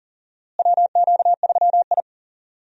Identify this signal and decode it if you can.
Morse code, heard as a keyed tone, about 40 wpm, 700 Hz. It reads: WQ3I
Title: USA Call Signs - 4 Characters - Repeated 2x - 40wpm